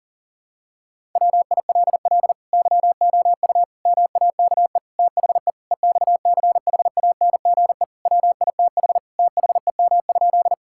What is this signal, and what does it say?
Morse code, heard as a keyed tone, about 40 wpm, 700 Hz. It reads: WILL YOU MAKE THE EXCHANGE WITH THEM?
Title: Sentences from Top 1000 Words - Review Part 6 - 40wpm